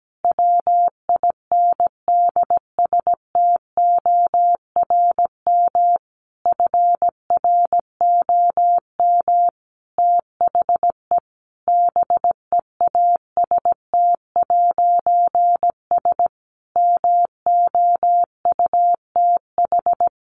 Morse code, heard as a keyed tone, about 17 wpm, 700 Hz. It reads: WINDSTORM FROM THE BEAST'S MOUTH
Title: A Sound Of Thunder 17wpm